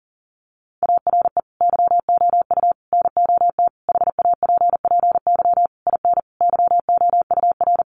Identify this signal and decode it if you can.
Morse code, heard as a keyed tone, about 40 wpm, 700 Hz. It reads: ARE YOU NOT HAPPY IN YOUR
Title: Sentences from Top 1000 Words 40wpm